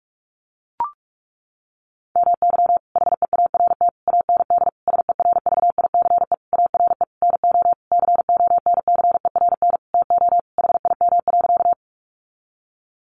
Morse code, heard as a keyed tone, about 44 wpm, 700 Hz. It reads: MY HEART AND SERVICE ARE NO CONCERN TO HIM.
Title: Sentences from Top 1200 Words 45wpm